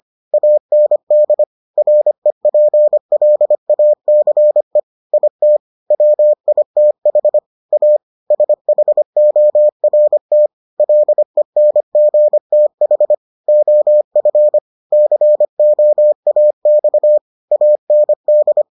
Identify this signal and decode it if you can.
Morse code, heard as a keyed tone, about 25 wpm, 600 Hz. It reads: AND REPLACE IT WITH A SHORT LENGTH OF COAX AND